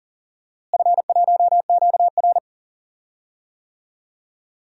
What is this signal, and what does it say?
Morse code, heard as a keyed tone, about 40 wpm, 700 Hz. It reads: F1QR
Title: DXCC-100 Call Signs - 4 Characters - Repeated 3x - 40wpm